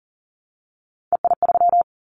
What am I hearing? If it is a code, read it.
Morse code, heard as a keyed tone, about 38 wpm, 700 Hz. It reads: EI3
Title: Callsign Prefixes for Top 50 DXCC - 40wpm